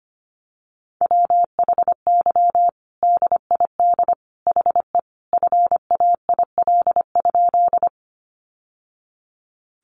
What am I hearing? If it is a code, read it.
Morse code, heard as a keyed tone, about 25 wpm, 700 Hz. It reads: WHY DID HE FAIL?
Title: Sentences from Top 1100 Words 25wpm